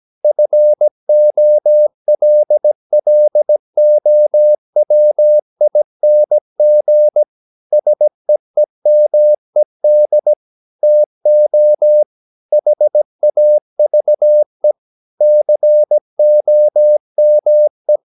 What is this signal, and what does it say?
Morse code, heard as a keyed tone, about 17 wpm, 600 Hz. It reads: FOLLOWING SEEMED TO HAVE COME
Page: Romanen Never af Ken Follet på engelsk.